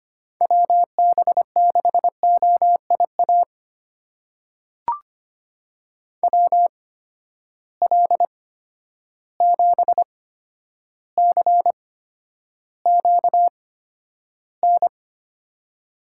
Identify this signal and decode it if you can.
Morse code, heard as a keyed tone, about 25 wpm, spaced as for 11 wpm, 700 Hz. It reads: WB6OIA WL7CQN
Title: USA Call Signs - 6 Characters - 8x Character Spacing - 25wpm